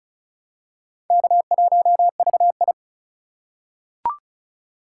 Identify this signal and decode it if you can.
Morse code, heard as a keyed tone, about 34 wpm, 700 Hz. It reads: K1VI
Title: USA Call Signs - 4 Characters - 3x Character Spacing - 35wpm